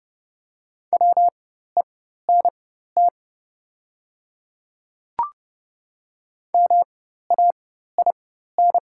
Morse code, heard as a keyed tone, about 31 wpm, spaced as for 14 wpm, 700 Hz. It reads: WENT MAIN
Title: 4x Character Spacing - Top 500 Words - 30wpm